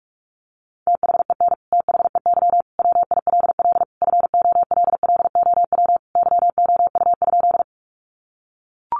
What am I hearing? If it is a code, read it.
Morse code, heard as a keyed tone, about 44 wpm, 700 Hz. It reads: THEN THEY WILL FOLLOW YOU?
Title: Sentences from Top 200 Words 45wpm